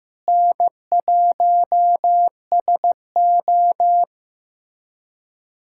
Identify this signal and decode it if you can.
Morse code, heard as a keyed tone, about 15 wpm, 700 Hz. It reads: N1SO
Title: USA Callsign DE Callsign 15wpm